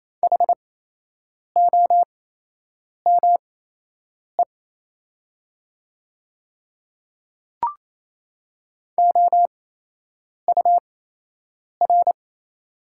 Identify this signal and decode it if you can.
Morse code, heard as a keyed tone, about 28 wpm, spaced as for 8 wpm, 700 Hz. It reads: HOME OUR
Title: 8x Character Spacing - Top 200 Words - 28wpm